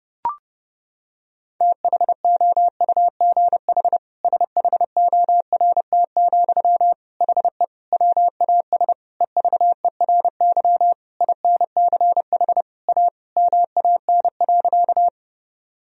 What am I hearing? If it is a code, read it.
Morse code, heard as a keyed tone, about 30 wpm, 700 Hz. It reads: THOUGH SHORT, HE WAS EVERY INCH A MAN.
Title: Sentences from Top 1000 Words - Review Part 3 - 30wpm